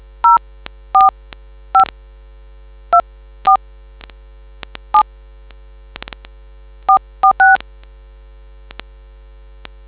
DTMF, heard as telephone touch tones, tones of about 110 ms. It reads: 04524 7 446